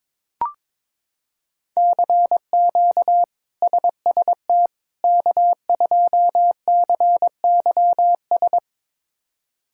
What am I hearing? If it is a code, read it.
Morse code, heard as a keyed tone, about 22 wpm, 700 Hz. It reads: CQ SST K2CYS
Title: K1USN SST QSO Practice - 22wpm